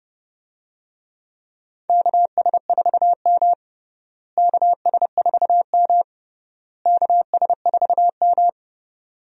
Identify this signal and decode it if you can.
Morse code, heard as a keyed tone, about 29 wpm, 700 Hz. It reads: KS4M KS4M KS4M
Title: USA Call Signs - 4 Characters - Repeated 3x - 30wpm